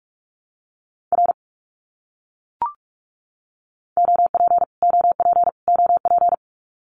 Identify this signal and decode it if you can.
Morse code, heard as a keyed tone, about 44 wpm, 700 Hz. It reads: R OP OP OP
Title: Abbreviations Repeated 3x - 45wpm